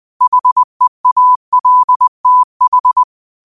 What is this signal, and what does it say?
Morse code, heard as a keyed tone, about 20 wpm, 1000 Hz. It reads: HEALTH
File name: qotdfiles / 20wpm